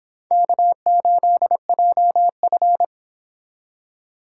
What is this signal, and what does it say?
Morse code, heard as a keyed tone, about 26 wpm, 700 Hz. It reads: K8JF